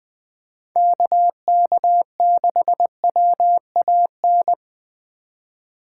Morse code, heard as a keyed tone, about 24 wpm, 700 Hz. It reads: KK6WAN
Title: Speed Racing - USA Call Signs 20wpm